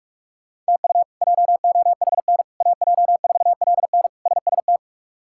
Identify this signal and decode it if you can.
Morse code, heard as a keyed tone, about 46 wpm, 700 Hz. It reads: TU JOHN AJ4LN SST
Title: K1USN SST QSO Practice - 45wpm